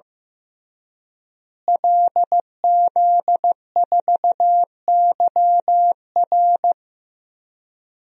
Morse code, heard as a keyed tone, about 15 wpm, 700 Hz. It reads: LZ4YR
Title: DXCC-100 Call Signs - 5 Characters - Repeated 3x - 15wpm